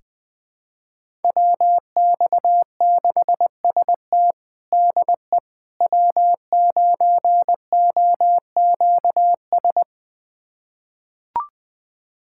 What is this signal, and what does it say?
Morse code, heard as a keyed tone, about 20 wpm, 700 Hz. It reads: WX6ST DE W9OQS
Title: USA Callsign DE Callsign - 5 Characters - 20wpm